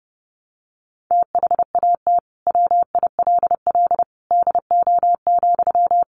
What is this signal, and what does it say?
Morse code, heard as a keyed tone, about 30 wpm, 700 Hz. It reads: THAT WILL DO,
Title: Sentences from Top 100 Words 30wpm